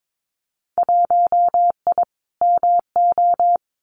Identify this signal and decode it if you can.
Morse code, heard as a keyed tone, about 22 wpm, 700 Hz. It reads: 1I MO
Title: Winter Field Day Exchanges Encore 22wpm